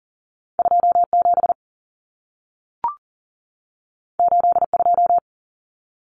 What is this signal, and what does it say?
Morse code, heard as a keyed tone, about 40 wpm, 700 Hz. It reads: 27 82
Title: Numbers - 2-Digit 40wpm